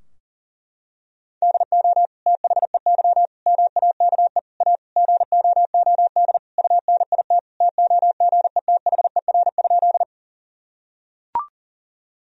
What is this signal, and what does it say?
Morse code, heard as a keyed tone, about 40 wpm, 700 Hz. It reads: DO THEY MAKE A GOOD UNIT TOGETHER?
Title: Sentences from Top 1000 Words - Review Part 2 - 40wpm